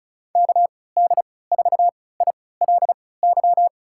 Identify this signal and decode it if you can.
Morse code, heard as a keyed tone, about 36 wpm, spaced as for 20 wpm, 700 Hz. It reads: KD4ILY
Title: USA Call Signs - 6 Characters - 3x Character Spacing - 35wpm